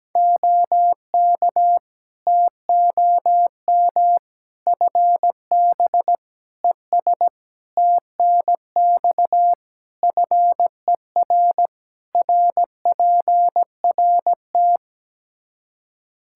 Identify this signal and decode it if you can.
Morse code, heard as a keyed tone, about 17 wpm, 700 Hz. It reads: OK TOM FB ES TNX FER RPRT
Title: Practice QSO - Exchange 2 - Info - 17wpm